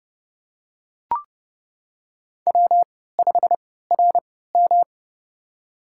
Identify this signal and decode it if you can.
Morse code, heard as a keyed tone, about 30 wpm, spaced as for 17 wpm, 700 Hz. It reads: W5RM
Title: USA Call Signs - 4 Characters - 3x Character Spacing - 30wpm